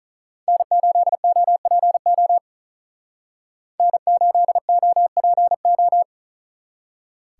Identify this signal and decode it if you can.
Morse code, heard as a keyed tone, about 46 wpm, 700 Hz. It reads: N8OPO N8OPO